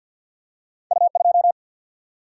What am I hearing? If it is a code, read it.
Morse code, heard as a keyed tone, about 50 wpm, 700 Hz. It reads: U 2
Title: Single-Letter Single-Number 50wpm